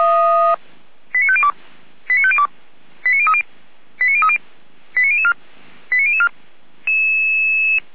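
Five-tone selective call (ZVEI), heard as quick runs of five tones, tones of about 80 ms. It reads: 89502 89502 89030 89030 89004 89004
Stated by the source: ZVEI five-tone call received by SDR at Feuerwehr Orlamünde.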